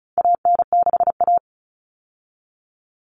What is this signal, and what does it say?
Morse code, heard as a keyed tone, about 34 wpm, 700 Hz. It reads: AN6A
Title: All Cut Numbers - 4-Digit 35wpm